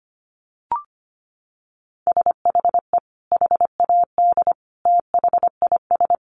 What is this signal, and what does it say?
Morse code, heard as a keyed tone, about 25 wpm, 700 Hz. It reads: SHE HAD THIS
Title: Sentences from Top 1200 Words 25wpm